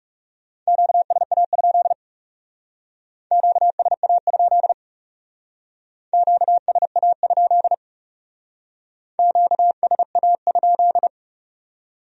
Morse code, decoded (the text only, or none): QSA? QSA? QSA? QSA?